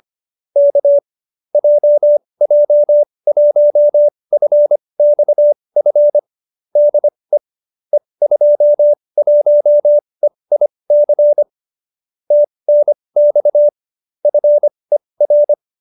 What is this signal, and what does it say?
Morse code, heard as a keyed tone, about 25 wpm, spaced as for 20 wpm, 590 Hz. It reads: K JJ1FXF DE E21EIC TNX FER